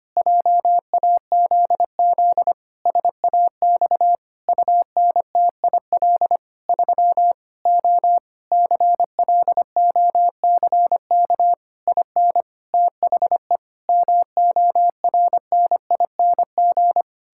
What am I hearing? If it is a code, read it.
Morse code, heard as a keyed tone, about 25 wpm, 700 Hz. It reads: JAZZ SAX UNTIL 3 O CLOCK IN THE MORNING
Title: Pangram Challenge v2 - 25wpm